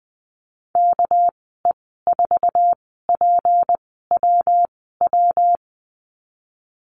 Morse code, heard as a keyed tone, about 20 wpm, spaced as for 15 wpm, 700 Hz. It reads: KE4PWW
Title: 231 Lesson 37g - Letter x - Call Signs 20wpm